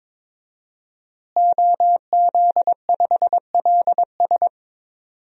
Morse code, heard as a keyed tone, about 22 wpm, 700 Hz. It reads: OZ5LS
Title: DXCC-100 Call Signs - 5 Characters - Repeated 3x - 22wpm